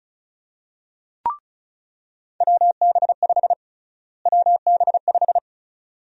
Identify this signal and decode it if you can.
Morse code, heard as a keyed tone, about 34 wpm, 700 Hz. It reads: WB5 WB5
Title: USA Call Sign Prefixes - Repeated 2x - 35wpm